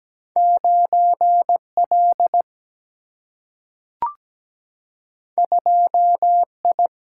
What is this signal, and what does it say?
Morse code, heard as a keyed tone, about 17 wpm, 700 Hz. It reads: 9L 2I
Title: Single-Letter Single-Number 17wpm